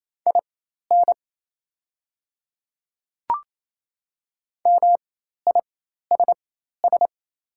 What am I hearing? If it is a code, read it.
Morse code, heard as a keyed tone, about 27 wpm, spaced as for 13 wpm, 700 Hz. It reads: IN MISS
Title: 4x Character Spacing - Top 500 Words - 28wpm